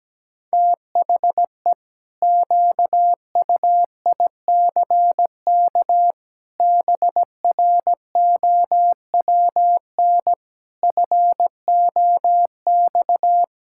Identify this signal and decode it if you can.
Morse code, heard as a keyed tone, about 17 wpm, 700 Hz. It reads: THE QUICK BROWN FOX
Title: Pangram Challenge v2 - 17wpm